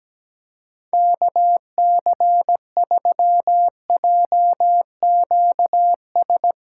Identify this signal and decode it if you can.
Morse code, heard as a keyed tone, about 17 wpm, 700 Hz. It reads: KC3JQS
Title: USA Callsign DE Callsign 17wpm